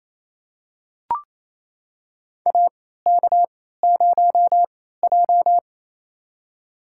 Morse code, heard as a keyed tone, about 28 wpm, spaced as for 16 wpm, 700 Hz. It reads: AK0J
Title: USA Call Signs - 4 Characters - 3x Character Spacing - 28wpm